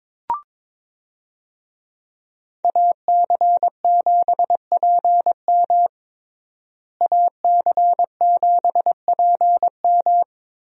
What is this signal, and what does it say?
Morse code, heard as a keyed tone, about 22 wpm, 700 Hz. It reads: AC7PM AC7PM
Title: USA Call Signs - 5 Characters - Repeated 2x - 22wpm